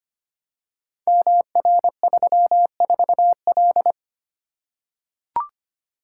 Morse code, heard as a keyed tone, about 25 wpm, 700 Hz. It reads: MR34L
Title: Code Groups - 5 Letter:Numbers - 25wpm